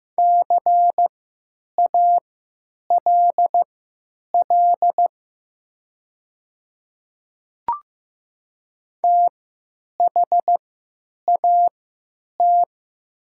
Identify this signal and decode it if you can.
Morse code, heard as a keyed tone, about 15 wpm, spaced as for 9 wpm, 700 Hz. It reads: CALL THAT
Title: 3x Character Spacing - Top 200 Words - 15wpm